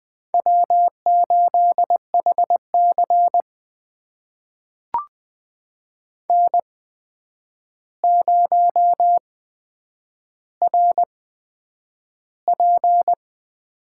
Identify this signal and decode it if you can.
Morse code, heard as a keyed tone, about 20 wpm, spaced as for 9 wpm, 700 Hz. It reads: W8HC N0RP
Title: USA Call Signs - 4 Characters - 8x Character Spacing - 20wpm